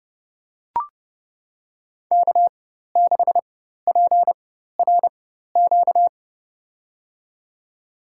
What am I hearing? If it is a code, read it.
Morse code, heard as a keyed tone, about 30 wpm, spaced as for 14 wpm, 700 Hz. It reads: K6PRQ
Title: USA Call Signs - 5 Characters - 4x Character Spacing - 30wpm